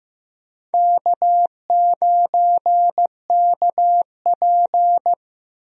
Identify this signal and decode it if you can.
Morse code, heard as a keyed tone, about 15 wpm, 700 Hz. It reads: K9KP
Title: USA Call Signs - 4 Characters - Repeated 2x - 15wpm